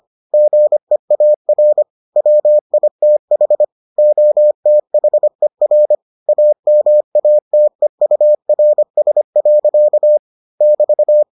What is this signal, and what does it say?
Morse code, heard as a keyed tone, about 25 wpm, 600 Hz. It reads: GEAR WITH OTHER AMATEURS. <BT>